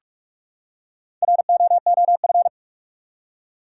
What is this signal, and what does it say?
Morse code, heard as a keyed tone, about 43 wpm, 700 Hz. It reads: ROOF
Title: Four Letter Words - 45wpm